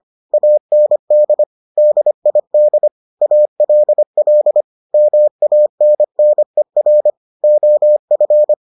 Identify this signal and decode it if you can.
Morse code, heard as a keyed tone, about 25 wpm, 600 Hz. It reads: AND DID ALL MANNER OF